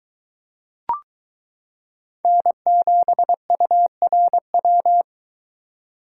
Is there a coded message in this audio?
Yes: Morse code from 2.2 to 5.0 s, heard as a keyed tone, about 22 wpm, 700 Hz. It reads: N7URW